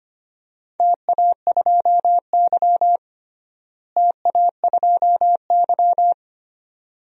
Morse code, heard as a keyed tone, about 25 wpm, 700 Hz. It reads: TA2Y TA2Y
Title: DXCC-100 Call Signs - 4 Characters - Repeated 2x - 25wpm